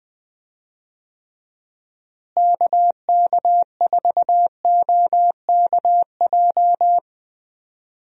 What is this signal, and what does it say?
Morse code, heard as a keyed tone, about 21 wpm, 700 Hz. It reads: KK4OKJ
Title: USA Call Signs - 6 Characters - Repeated 2x - 20wpm